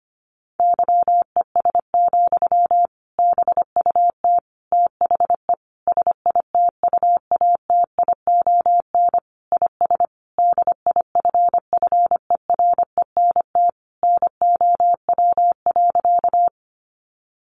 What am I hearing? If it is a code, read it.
Morse code, heard as a keyed tone, about 25 wpm, 700 Hz. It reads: YES, BUT THE SITUATION IS DIFFERENT NOW.
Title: Sentences from Top 1200 Words 25wpm